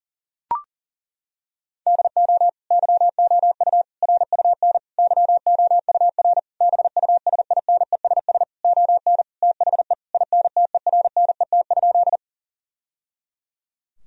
Morse code, heard as a keyed tone, about 40 wpm, 700 Hz. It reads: DO YOU RUN YOUR BUSINESS ON THE INTERNET?
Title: Sentences from Top 1000 Words - Review Part 4 - 40wpm